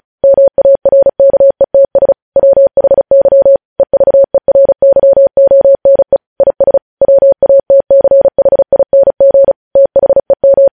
Morse code, heard as a keyed tone, about 36 wpm, 550 Hz. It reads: MARKETS WHY EVERYONE IS WATCHING THEM
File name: podcast / News headlines-1x-35x35